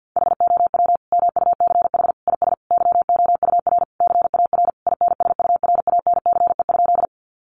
Morse code, heard as a keyed tone, about 50 wpm, 700 Hz. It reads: HOW MUCH IS YOUR CAR INSURANCE?
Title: Sentences from Top 1100 Words 50wpm